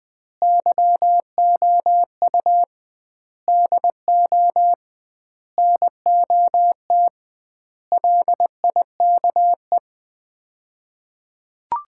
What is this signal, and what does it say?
Morse code, heard as a keyed tone, about 21 wpm, 700 Hz. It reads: YOU DO NOT LIKE
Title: 2x Word-Spacing - Sets of 5 Words - 20wpm